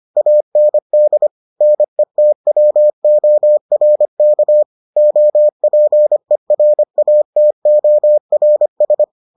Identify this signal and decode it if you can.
Morse code, heard as a keyed tone, about 25 wpm, 600 Hz. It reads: AND NETWORK OPERATORS